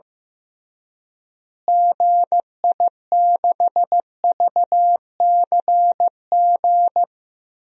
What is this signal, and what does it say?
Morse code, heard as a keyed tone, about 15 wpm, 700 Hz. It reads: GI6VCG
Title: DXCC-100 Call Signs - 6 Characters - Repeated 2x - 15wpm